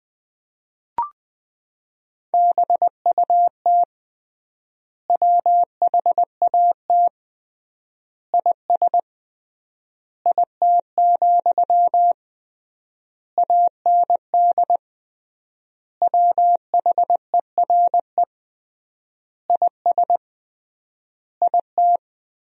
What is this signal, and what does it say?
Morse code, heard as a keyed tone, about 20 wpm, 700 Hz. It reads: BUT WHAT IS IT, AND WHERE IS IT
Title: 3x Word-Spacing - Sentences from Top 300 Words - 20wpm